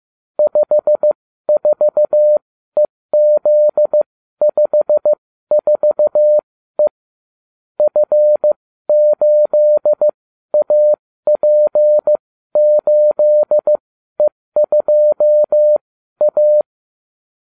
Morse code, heard as a keyed tone, about 15 wpm, spaced as for 12 wpm, 600 Hz. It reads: 54EZ54E F8AP8E2A